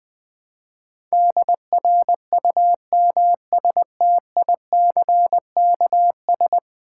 Morse code, heard as a keyed tone, about 20 wpm, 700 Hz. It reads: DRUMSTICKS
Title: Easy Words with Hints - 20wpm